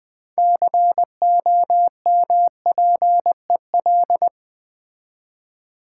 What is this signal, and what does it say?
Morse code, heard as a keyed tone, about 20 wpm, 690 Hz. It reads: COMPEL
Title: Build A Word - 3 Letters At A Time - 6-Letter Words - 20wpm